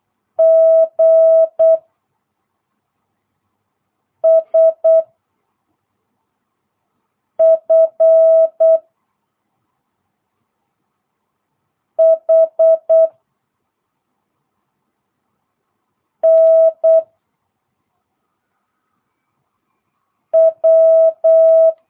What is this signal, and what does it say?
Morse code, heard as a keyed tone, about 8 wpm, 660 Hz. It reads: G S F H N W